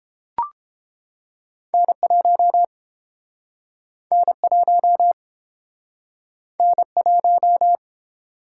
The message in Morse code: N1 N1 N1

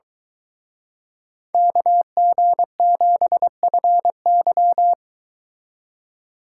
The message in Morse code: KG7FY

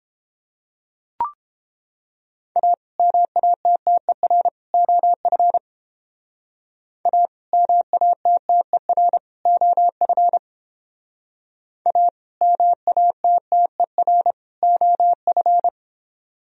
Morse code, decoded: A MATTER OF A MATTER OF A MATTER OF

